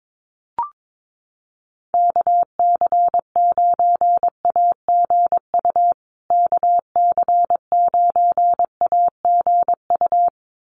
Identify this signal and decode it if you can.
Morse code, heard as a keyed tone, about 22 wpm, 700 Hz. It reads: KC9AGU KC9AGU
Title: USA Call Signs Repeated 2x 22wpm